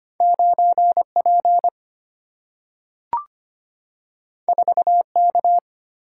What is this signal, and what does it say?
Morse code, heard as a keyed tone, about 25 wpm, 700 Hz. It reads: 9 P 4 K
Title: Single-Letter Single-Number 25wpm